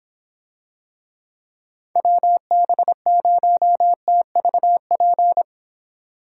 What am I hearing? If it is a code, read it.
Morse code, heard as a keyed tone, about 26 wpm, 700 Hz. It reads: WB0TVP